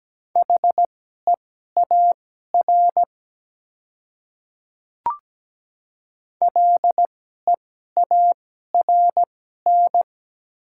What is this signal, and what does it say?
Morse code, heard as a keyed tone, about 17 wpm, spaced as for 12 wpm, 700 Hz. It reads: HEAR LEARN
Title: 2x Character Spacing - Top 300 Words - 17wpm